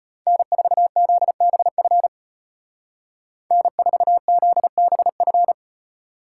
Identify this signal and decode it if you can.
Morse code, heard as a keyed tone, about 33 wpm, 700 Hz. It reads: N4ZBF N4ZBF